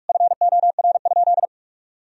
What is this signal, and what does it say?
Morse code, heard as a keyed tone, about 41 wpm, 700 Hz. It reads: FOR?
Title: Sentences from Top 1000 Words - Review Part 3 - 45wpm